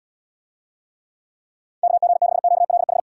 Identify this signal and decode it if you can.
Morse code, heard as a keyed tone, about 78 wpm, 700 Hz. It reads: KG6QLH